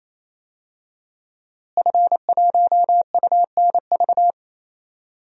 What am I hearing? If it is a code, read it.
Morse code, heard as a keyed tone, about 27 wpm, 700 Hz. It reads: F1UNV
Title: DXCC-100 Call Signs - 5 Characters - Repeated 3x - 28wpm